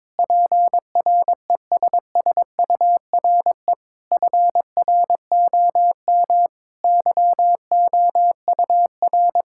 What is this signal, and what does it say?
Morse code, heard as a keyed tone, about 21 wpm, 700 Hz. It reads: PRESSURE FROM YOUR
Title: Sentences from Top 1000 Words - Review Part 6 - 22wpm